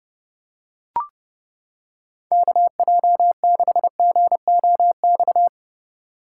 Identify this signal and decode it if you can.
Morse code, heard as a keyed tone, about 29 wpm, 700 Hz. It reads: KJ6GOX